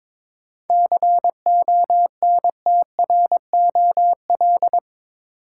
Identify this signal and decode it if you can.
Morse code, heard as a keyed tone, about 22 wpm, 700 Hz. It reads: CONTROL
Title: Sentences from Top 1000 Words - Review Part 2 - New Word Drill - 22wpm